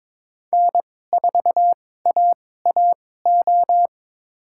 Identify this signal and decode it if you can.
Morse code, heard as a keyed tone, about 22 wpm, spaced as for 16 wpm, 700 Hz. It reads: N4AAO
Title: USA Call Signs - 2x Character Spacing - 22wpm